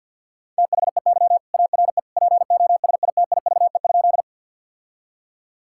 Morse code, heard as a keyed tone, about 50 wpm, 700 Hz. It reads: THEY ARE POSITIVE?
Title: Sentences from Top 1000 Words - Review Part 5 - 50wpm